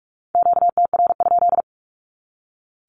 Morse code, heard as a keyed tone, about 44 wpm, 700 Hz. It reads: QTR?
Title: Q-Codes 45wpm